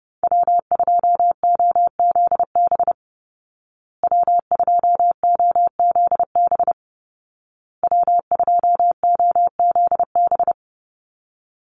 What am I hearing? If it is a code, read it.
Morse code, heard as a keyed tone, about 28 wpm, 700 Hz. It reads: W2OZB W2OZB W2OZB
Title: Call Signs Repeated 2x - 30wpm